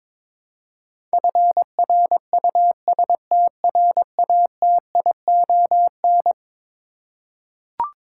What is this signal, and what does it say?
Morse code, heard as a keyed tone, about 22 wpm, 700 Hz. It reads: FRUSTRATION
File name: practice / 22wpm / Eleven Letter Words - 22wpm